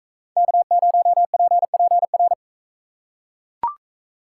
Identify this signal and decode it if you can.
Morse code, heard as a keyed tone, about 41 wpm, 700 Hz. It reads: K0PPR